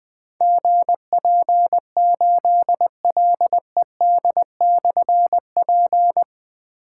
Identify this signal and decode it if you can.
Morse code, heard as a keyed tone, about 20 wpm, 700 Hz. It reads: GP8LED/P
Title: Difficult Call Signs 20wpm